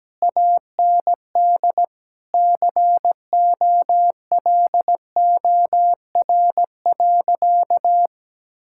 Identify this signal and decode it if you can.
Morse code, heard as a keyed tone, about 17 wpm, 700 Hz. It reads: AND COLOR.
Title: Sentences from Top 1600 Words 17wpm